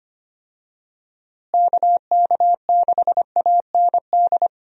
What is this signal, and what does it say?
Morse code, heard as a keyed tone, about 25 wpm, 700 Hz. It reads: KK6AND
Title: USA Call Signs - 6 Characters - Repeated 2x - 25wpm